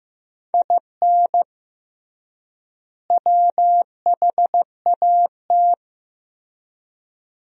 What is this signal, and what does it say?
Morse code, heard as a keyed tone, about 15 wpm, 700 Hz. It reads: IN WHAT
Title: 3x Word-Spacing - Sentences from Top 200 Words - 15wpm